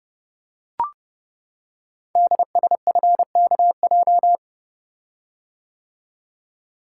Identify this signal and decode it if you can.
Morse code, heard as a keyed tone, about 30 wpm, 690 Hz. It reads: DSFKJ
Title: Code Groups - 5 Letters - 30wpm